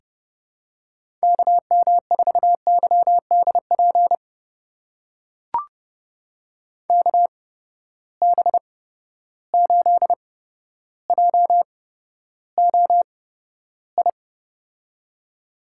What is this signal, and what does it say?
Morse code, heard as a keyed tone, about 29 wpm, spaced as for 13 wpm, 700 Hz. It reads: KM4YDP KB8JOI
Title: USA Call Signs - 6 Characters - 8x Character Spacing - 30wpm